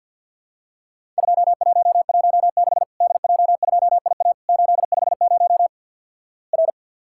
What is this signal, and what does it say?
Morse code, heard as a keyed tone, about 50 wpm, 700 Hz. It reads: 2116 DJ2IA 850 R
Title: Worked All Europe DX Contest - QTC Exchange - 50wpm